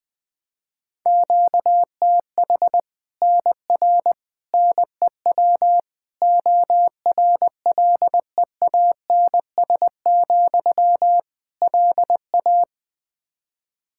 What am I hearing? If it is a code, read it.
Morse code, heard as a keyed tone, about 20 wpm, 700 Hz. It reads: QTH NR NEW ORLEANS, LA
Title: K1USN SST QSO Practice 20wpm